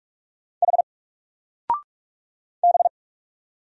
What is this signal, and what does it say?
Morse code, heard as a keyed tone, about 44 wpm, 700 Hz. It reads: H B
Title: Single Character (Letter:Number:Punctuation:Prosign) 45wpm